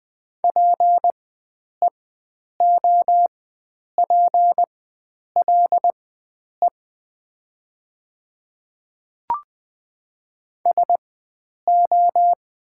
Morse code, heard as a keyed tone, about 20 wpm, spaced as for 9 wpm, 700 Hz. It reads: PEOPLE SO
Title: 4x Character Spacing - Top 100 Words - 20wpm